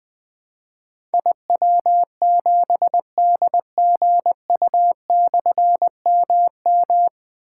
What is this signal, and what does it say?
Morse code, heard as a keyed tone, about 20 wpm, 700 Hz. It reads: IW7DGU/MM
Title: Difficult Call Signs 20wpm